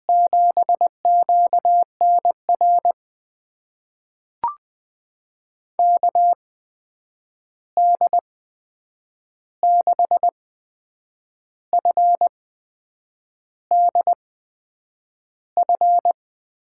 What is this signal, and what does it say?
Morse code, heard as a keyed tone, about 20 wpm, spaced as for 7 wpm, 700 Hz. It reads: 7QNR KD6FDF
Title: USA Call Signs - 6 Characters - 8x Character Spacing - 20wpm